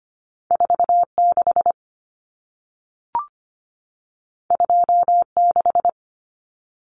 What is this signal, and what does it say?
Morse code, heard as a keyed tone, about 25 wpm, 700 Hz. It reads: 46 26
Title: Common Cut Numbers - 2-Digit 25wpm